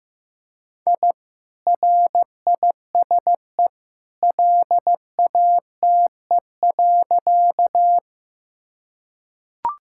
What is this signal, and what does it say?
Morse code, heard as a keyed tone, about 15 wpm, 700 Hz. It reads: I RISE LATE.
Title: Sentences from Top 1000 Words - Review Part 6 - 15wpm